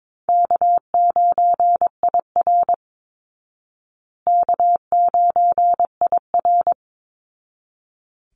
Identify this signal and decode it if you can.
Morse code, heard as a keyed tone, about 22 wpm, 700 Hz. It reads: K9IR K9IR
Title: Call Signs Repeated 3x - 22wpm